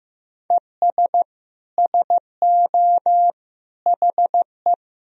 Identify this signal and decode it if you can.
Morse code, heard as a keyed tone, about 15 wpm, 700 Hz. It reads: ES SO HE
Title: Sentences from Top 500 Words and Abbreviations 15wpm